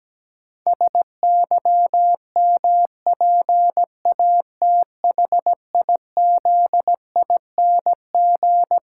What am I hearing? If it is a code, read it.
Morse code, heard as a keyed tone, about 17 wpm, 700 Hz. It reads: SYMPATHIZING
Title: Pangram Challenge v2 - 17wpm